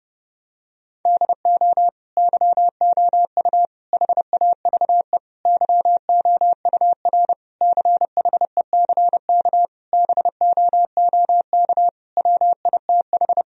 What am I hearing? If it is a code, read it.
Morse code, heard as a keyed tone, about 30 wpm, 700 Hz. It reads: DO YOU HAVE YOUR CHECK BOOK WITH
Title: Sentences from Top 1000 Words - Review Part 3 - 30wpm